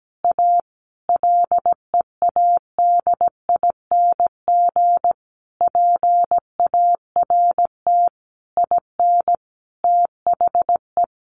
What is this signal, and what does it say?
Morse code, heard as a keyed tone, about 17 wpm, 700 Hz. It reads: A LEADING PART IN THE
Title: Sentences from Top 1100 Words 17wpm